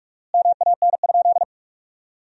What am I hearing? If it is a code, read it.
Morse code, heard as a keyed tone, about 45 wpm, 700 Hz. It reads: MAN?
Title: 4x Word-Spacing - Sentences from Top 200 Words - 45wpm